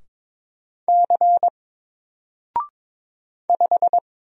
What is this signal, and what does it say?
Morse code, heard as a keyed tone, about 22 wpm, 700 Hz. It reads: C 5
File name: Single-Letter-Number-22wpm